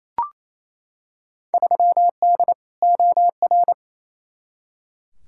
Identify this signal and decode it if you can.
Morse code, heard as a keyed tone, about 28 wpm, 700 Hz. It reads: 3D OR
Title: Field Day Exchanges Encore - 28wpm